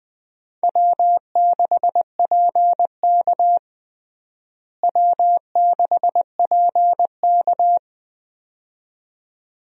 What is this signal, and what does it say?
Morse code, heard as a keyed tone, about 21 wpm, 700 Hz. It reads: W6PK W6PK
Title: USA Call Signs - 4 Characters - Repeated 2x - 20wpm